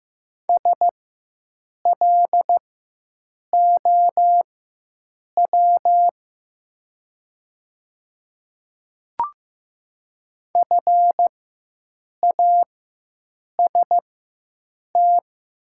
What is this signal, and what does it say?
Morse code, heard as a keyed tone, about 15 wpm, spaced as for 7 wpm, 700 Hz. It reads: SLOW FAST
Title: 4x Character Spacing - Top 500 Words - 15wpm